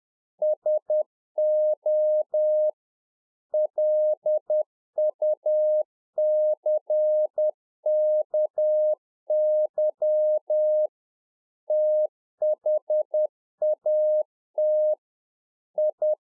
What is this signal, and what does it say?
Morse code, heard as a keyed tone, about 10 wpm, 610 Hz. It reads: SO LUCKY THAT I
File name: 2011-04-mp3 / A SMALL TRANCEIVER 1 10WPM